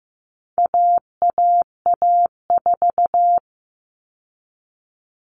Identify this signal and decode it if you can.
Morse code, heard as a keyed tone, about 15 wpm, 700 Hz. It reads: AAA4
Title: All Cut Numbers - 4-Digit 15wpm